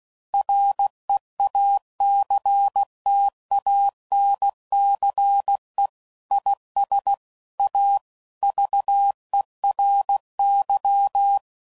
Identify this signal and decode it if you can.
Morse code, heard as a keyed tone, about 16 wpm, 800 Hz. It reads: REACTANCE IS A VERY
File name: CW Code Groups @15WPM / Technical Subjects / Reactance